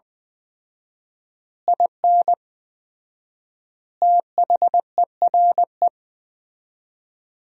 Morse code, heard as a keyed tone, about 20 wpm, 700 Hz. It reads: IN THERE
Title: 4x Word-Spacing - Sentences from Top 500 Words - 20wpm